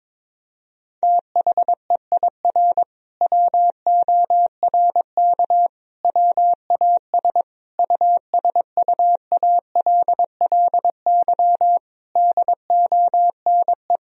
Morse code, heard as a keyed tone, about 21 wpm, 700 Hz. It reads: THEIR WORK WAS USUALLY DONE
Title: Sentences from Top 1000 Words - Review Part 3 - 22wpm